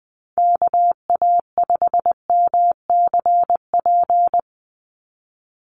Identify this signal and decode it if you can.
Morse code, heard as a keyed tone, about 20 wpm, 700 Hz. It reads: KA5MCP
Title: USA Call Signs 20wpm